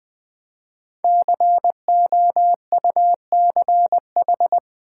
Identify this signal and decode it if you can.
Morse code, heard as a keyed tone, about 21 wpm, 700 Hz. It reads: COUCH
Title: Easy Words with Hints - 20wpm